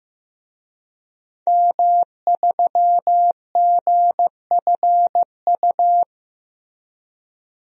Morse code, heard as a keyed tone, about 15 wpm, 700 Hz. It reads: M3GFU
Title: DXCC-100 Call Signs - 5 Characters - Repeated 2x - 15wpm